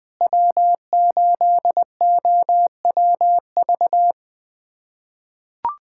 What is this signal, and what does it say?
Morse code, heard as a keyed tone, about 21 wpm, 700 Hz. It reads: W8OWV
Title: USA Callsign DE Callsign - 5 Characters - 20wpm